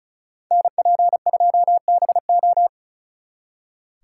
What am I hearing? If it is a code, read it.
Morse code, heard as a keyed tone, about 43 wpm, 700 Hz. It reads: NP2BO
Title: Speed Racing - USA 2x2 Call Signs 35wpm